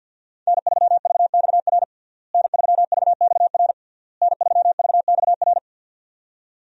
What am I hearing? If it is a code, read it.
Morse code, heard as a keyed tone, about 52 wpm, 700 Hz. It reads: N3VXR N3VXR N3VXR
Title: USA Call Signs - 5 Characters - Repeated 3x - 50wpm